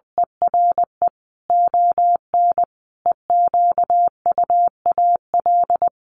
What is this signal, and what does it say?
Morse code, heard as a keyed tone, about 20 wpm, 700 Hz. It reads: ERE ON EQUAL
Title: Sentences from Top 1400 Words 20wpm